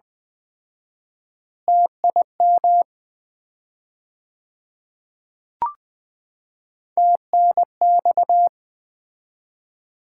Morse code, heard as a keyed tone, about 20 wpm, 700 Hz. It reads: TIM TNX
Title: Common QSO Phrases - 4x Word-Spacing - 20wpm